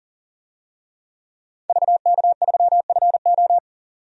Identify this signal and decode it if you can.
Morse code, heard as a keyed tone, about 40 wpm, 700 Hz. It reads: VK3FO
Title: DXCC-100 Call Signs - 5 Characters - 40wpm